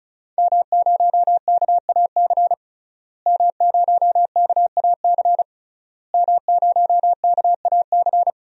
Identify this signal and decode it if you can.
Morse code, heard as a keyed tone, about 36 wpm, 700 Hz. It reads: M0KAC M0KAC M0KAC
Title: DXCC-100 Call Signs - 5 Characters - Repeated 3x - 35wpm